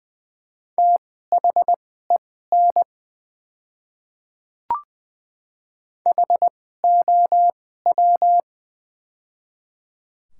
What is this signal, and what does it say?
Morse code, heard as a keyed tone, about 20 wpm, spaced as for 14 wpm, 700 Hz. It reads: THEN HOW
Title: 2x Character Spacing - Top 200 Words - 20wpm